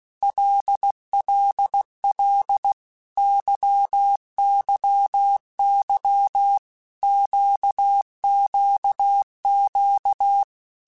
Morse code, heard as a keyed tone, about 16 wpm, 800 Hz. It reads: LLL YYY QQQ
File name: Basic Letter Groups By7Groups / LFCYQ / LFCYQ@15